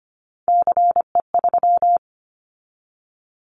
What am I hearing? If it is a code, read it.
Morse code, heard as a keyed tone, about 25 wpm, 700 Hz. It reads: CE3
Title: Callsign Prefixes for Top 100 DXCC - 25wpm